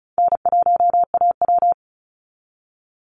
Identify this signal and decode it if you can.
Morse code, heard as a keyed tone, about 34 wpm, 700 Hz. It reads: N1AW
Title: NA SOTA Chaser Callsigns 35wpm